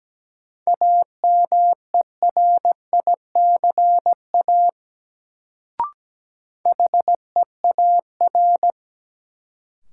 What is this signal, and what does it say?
Morse code, heard as a keyed tone, about 17 wpm, 700 Hz. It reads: AMERICA HEAR
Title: Sets of 5 Words - New Word Drill - 17wpm